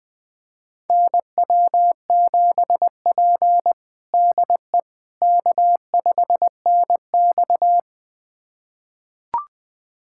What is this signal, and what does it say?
Morse code, heard as a keyed tone, about 20 wpm, 700 Hz. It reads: NW7P DE K5NX
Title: USA Callsign DE Callsign - 4 Characters - 20wpm